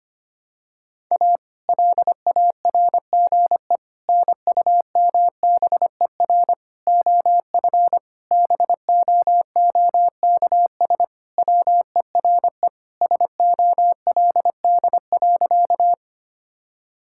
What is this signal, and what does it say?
Morse code, heard as a keyed tone, about 25 wpm, 700 Hz. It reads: A LARGE NUMBER OF BOOKS WERE SOLD.
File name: practice / 25wpm / Sentences with 4-Word Phrases - 25wpm